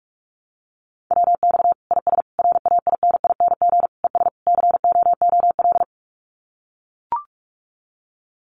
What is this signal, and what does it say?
Morse code, heard as a keyed tone, about 44 wpm, 700 Hz. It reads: WX IS RAINING ES COOL
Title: Common QSO Phrases 45wpm